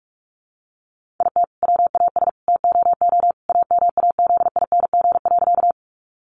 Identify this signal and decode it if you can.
Morse code, heard as a keyed tone, about 44 wpm, 700 Hz. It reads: IT WAS TOO AMAZING.
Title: Sentences from Top 1500 Words 45wpm